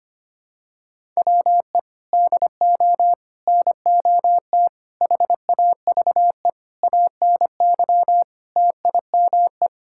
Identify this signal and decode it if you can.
Morse code, heard as a keyed tone, about 25 wpm, 700 Hz. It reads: WE DO NOT HAVE ANY TIME
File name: practice / 25wpm / Sentences with 4-Word Phrases - 25wpm